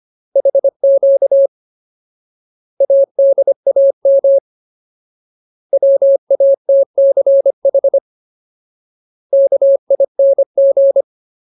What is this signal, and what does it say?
Morse code, heard as a keyed tone, about 25 wpm, 550 Hz. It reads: HQ ADAM WATCH KING
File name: Ultimate Word List / 25WPM Ultimate